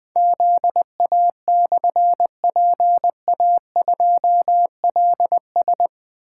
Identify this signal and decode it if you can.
Morse code, heard as a keyed tone, about 20 wpm, 700 Hz. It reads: ZA/PA2LS
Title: Difficult Call Signs 20wpm